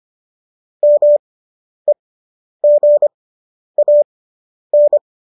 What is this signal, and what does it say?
Morse code, heard as a keyed tone, about 25 wpm, spaced as for 10 wpm, 590 Hz. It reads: MEGAN